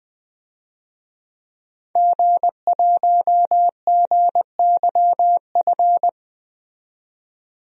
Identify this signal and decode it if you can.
Morse code, heard as a keyed tone, about 20 wpm, 700 Hz. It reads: G1GYF
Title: DXCC-100 Call Signs - 5 Characters - 20wpm